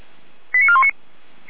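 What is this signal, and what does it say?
Five-tone selective call (ZVEI), heard as quick runs of five tones, tones of about 80 ms. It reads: 89419
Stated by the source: ZVEI five-tone call received by SDR, code 89025.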